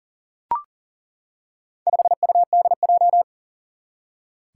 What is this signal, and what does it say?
Morse code, heard as a keyed tone, about 39 wpm, 700 Hz. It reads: 5UDJ
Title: Four Challenging Letter-Number Combinations 40wpm